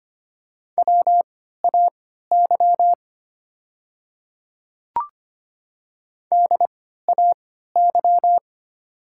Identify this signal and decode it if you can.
Morse code, heard as a keyed tone, about 25 wpm, spaced as for 14 wpm, 700 Hz. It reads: WAY DAY
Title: 3x Character Spacing - Top 100 Words - 25wpm